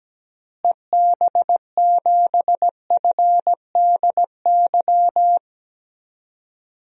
Morse code, heard as a keyed tone, about 17 wpm, 700 Hz. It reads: EB7FDY
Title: DXCC-100 Call Signs - 6 Characters - Repeated 3x - 17wpm